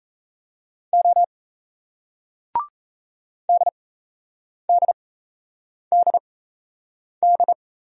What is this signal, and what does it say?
Morse code, heard as a keyed tone, about 38 wpm, spaced as for 8 wpm, 700 Hz. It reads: O DDDD